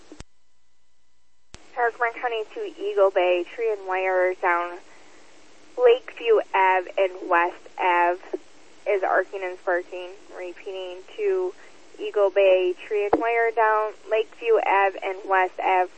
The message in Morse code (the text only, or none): none